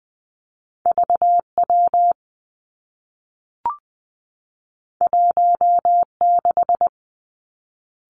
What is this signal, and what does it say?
Morse code, heard as a keyed tone, about 20 wpm, 700 Hz. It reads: V W 1 6
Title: Two Challenging Letter-Number Combinations 20wpm